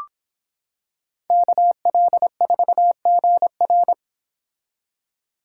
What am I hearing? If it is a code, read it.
Morse code, heard as a keyed tone, about 25 wpm, 700 Hz. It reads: KL4GR